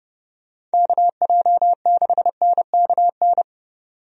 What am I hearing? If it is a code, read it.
Morse code, heard as a keyed tone, about 30 wpm, 700 Hz. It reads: KJ6NKN
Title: USA Call Signs - 6 Characters - 30wpm